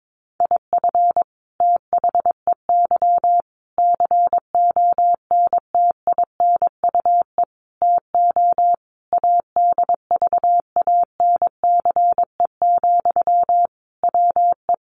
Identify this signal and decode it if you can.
Morse code, heard as a keyed tone, about 22 wpm, 700 Hz. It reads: IF THEY CONTINUE TO ADVANCE, WE
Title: Sentences from Top 1400 Words 22wpm